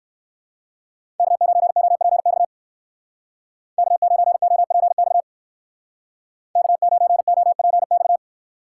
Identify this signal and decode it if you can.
Morse code, heard as a keyed tone, about 54 wpm, 700 Hz. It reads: K9OPX K9OPX K9OPX
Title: Speed Racing - USA 1x3 Call Signs 45wpm